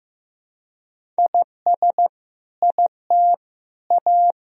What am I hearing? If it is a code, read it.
Morse code, heard as a keyed tone, about 15 wpm, 700 Hz. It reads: IS IT A
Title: Sentences from Top 1000 Words - Review Part 2 - 15wpm